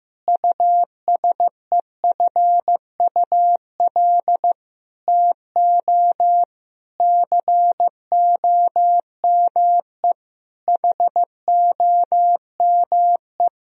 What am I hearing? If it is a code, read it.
Morse code, heard as a keyed tone, about 15 wpm, 700 Hz. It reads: USEFUL TO COME HOME
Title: Sentences from Top 1000 Words - Review Part 4 - 15wpm